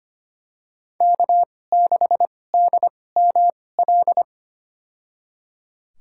Morse code, heard as a keyed tone, about 25 wpm, spaced as for 18 wpm, 700 Hz. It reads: K6DML
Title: USA Call Signs - 5 Characters - 2x Character Spacing - 25wpm